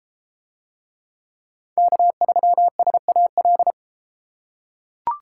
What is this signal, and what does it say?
Morse code, heard as a keyed tone, about 32 wpm, 700 Hz. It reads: K3SAL